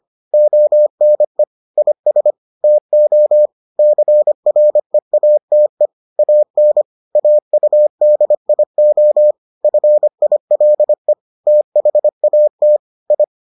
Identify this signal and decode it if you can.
Morse code, heard as a keyed tone, about 25 wpm, 600 Hz. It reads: ONE IS TO CREATE AN AUDIO FILE THAT I